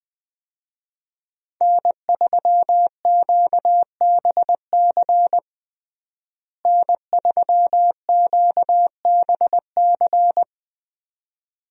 Morse code, heard as a keyed tone, about 20 wpm, 700 Hz. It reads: N3QBC N3QBC
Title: USA Call Signs - 5 Characters - Repeated 2x - 20wpm